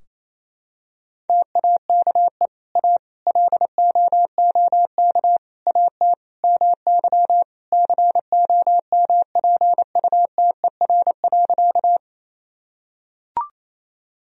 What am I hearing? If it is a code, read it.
Morse code, heard as a keyed tone, about 28 wpm, 690 Hz. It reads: TAKE A LOOK AT MY COMPUTER.
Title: Sentences from Top 1000 Words - Review Part 3 - 28wpm